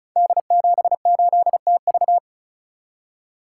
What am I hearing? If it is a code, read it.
Morse code, heard as a keyed tone, about 36 wpm, 700 Hz. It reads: D78TV
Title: Code Groups - 5 Letter:Numbers - 35wpm